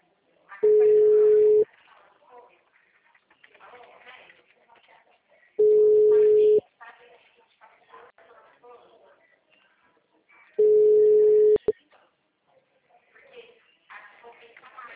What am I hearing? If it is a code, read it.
Ringback tone (Europe, 425 Hz), with speech.
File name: out- -2024